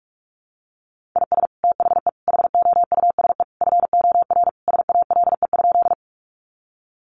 Morse code, heard as a keyed tone, about 44 wpm, 700 Hz. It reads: IS THE HOUSE FOR SALE?
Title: Sentences from Top 1400 Words 45wpm